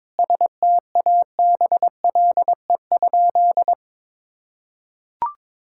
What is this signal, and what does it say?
Morse code, heard as a keyed tone, about 21 wpm, 700 Hz. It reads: STABLE?
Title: Sentences from Top 1600 Words 22wpm